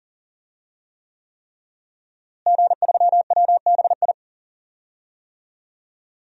Morse code, heard as a keyed tone, about 37 wpm, 700 Hz. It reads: G3WBI
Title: DXCC-100 Call Signs - 5 Characters - Repeated 3x - 40wpm